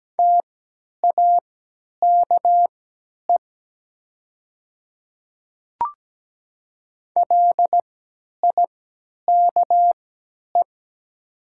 Morse code, heard as a keyed tone, about 17 wpm, spaced as for 10 wpm, 700 Hz. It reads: TAKE LIKE
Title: 3x Character Spacing - Top 200 Words - 17wpm